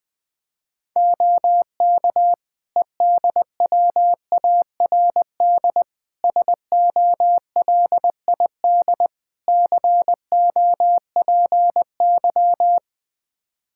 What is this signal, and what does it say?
Morse code, heard as a keyed tone, about 20 wpm, 700 Hz. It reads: OK EDWARD SOLID COPY
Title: Practice QSO - Exchange 3 - More Info - 20wpm